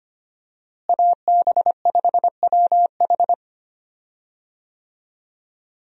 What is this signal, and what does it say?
Morse code, heard as a keyed tone, about 25 wpm, 700 Hz. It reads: AB5WH
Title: USA Call Signs - 5 Characters - Repeated 2x - 25wpm